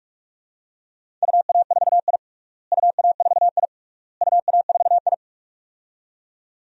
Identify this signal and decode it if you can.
Morse code, heard as a keyed tone, about 46 wpm, 700 Hz. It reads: UA4I UA4I UA4I
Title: DXCC-100 Call Signs - 4 Characters - Repeated 3x - 45wpm